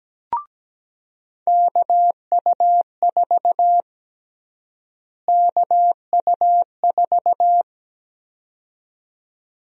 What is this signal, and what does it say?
Morse code, heard as a keyed tone, about 17 wpm, 700 Hz. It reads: KU4 KU4
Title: USA Call Sign Prefixes - Repeated 2x - 17wpm